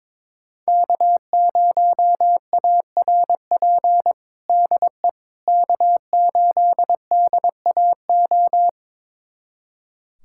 Morse code, heard as a keyed tone, about 23 wpm, 700 Hz. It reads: K0ARP DE K8DAO
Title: USA Callsign DE Callsign - 5 Characters - 22wpm